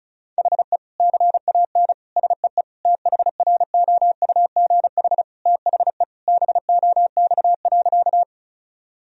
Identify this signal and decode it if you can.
Morse code, heard as a keyed tone, about 36 wpm, 700 Hz. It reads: HE CAN SEE THROUGH THE BOX.
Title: Sentences from Top 1000 Words - Review Part 2 - 35wpm